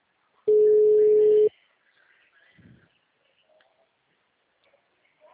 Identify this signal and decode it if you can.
Ringback tone (Europe, 425 Hz).
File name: out- -2016- - - .27